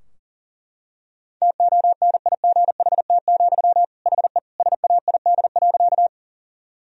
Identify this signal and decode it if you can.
Morse code, heard as a keyed tone, about 40 wpm, 700 Hz. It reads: TONIGHT, HE SAID.
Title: Sentences from Top 1000 Words - Review Part 3 - 40wpm